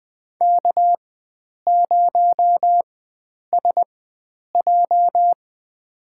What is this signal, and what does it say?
Morse code, heard as a keyed tone, about 20 wpm, spaced as for 9 wpm, 700 Hz. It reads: K0SJ
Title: USA Call Signs - 4 Characters - 4x Character Spacing - 20wpm